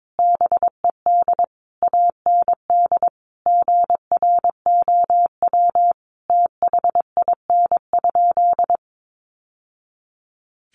Morse code, heard as a keyed tone, about 22 wpm, 700 Hz. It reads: BED AND GROW THIN?
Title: Sentences from Top 1200 Words 22wpm